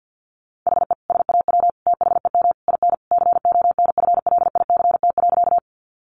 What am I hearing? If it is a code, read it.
Morse code, heard as a keyed tone, about 50 wpm, 700 Hz. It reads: HE SAW THEM IN CONFLICT.
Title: Sentences from Top 1300 Words 50wpm